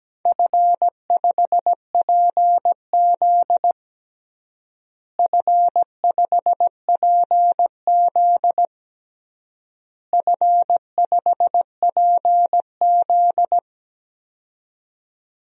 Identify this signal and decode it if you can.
Morse code, heard as a keyed tone, about 17 wpm, 700 Hz. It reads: F5PZ F5PZ F5PZ
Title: DXCC-100 Call Signs - 4 Characters - Repeated 3x - 17wpm